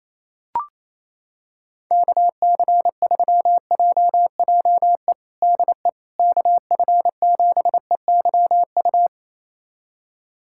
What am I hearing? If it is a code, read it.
Morse code, heard as a keyed tone, about 28 wpm, 700 Hz. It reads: KC3JJE DE KF7EYU
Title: USA Callsign DE Callsign 28wpm